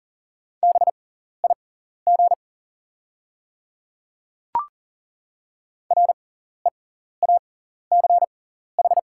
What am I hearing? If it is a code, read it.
Morse code, heard as a keyed tone, about 40 wpm, spaced as for 14 wpm, 690 Hz. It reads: BIG REACH
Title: 6x Character Spacing - Top 500 Words - 40wpm